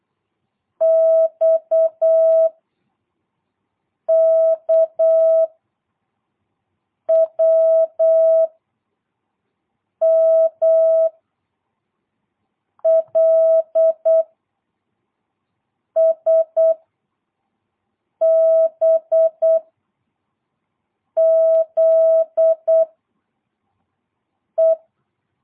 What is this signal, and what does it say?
Morse code, heard as a keyed tone, about 8 wpm, 660 Hz. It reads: X K W M L S B Z E
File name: jh8cbh / 2022cw